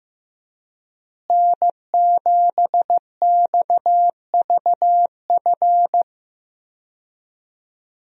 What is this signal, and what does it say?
Morse code, heard as a keyed tone, about 15 wpm, 700 Hz. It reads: N7XVF
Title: USA Call Signs - 5 Characters - Repeated 2x - 15wpm